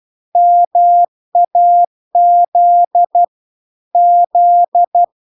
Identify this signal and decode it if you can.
Morse code, heard as a keyed tone, about 12 wpm, 700 Hz. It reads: MAZ Z